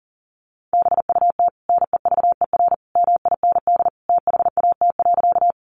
Morse code, heard as a keyed tone, about 40 wpm, 700 Hz. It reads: BUT NEVER MIND THAT.
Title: Sentences from Top 500 Words 40wpm